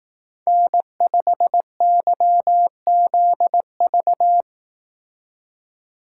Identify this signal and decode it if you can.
Morse code, heard as a keyed tone, about 18 wpm, 700 Hz. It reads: N5YZV
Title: Speed Racing - USA 1x3 Call Signs 15wpm